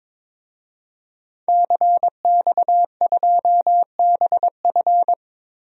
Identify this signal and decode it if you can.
Morse code, heard as a keyed tone, about 22 wpm, 700 Hz. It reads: CX2BF
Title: DXCC-100 Call Signs - 5 Characters - 22wpm